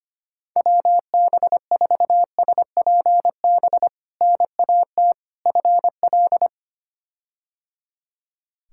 Morse code, heard as a keyed tone, about 25 wpm, 700 Hz. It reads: WB4SPB NAT FL
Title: K1USN SST QSO Practice - 25wpm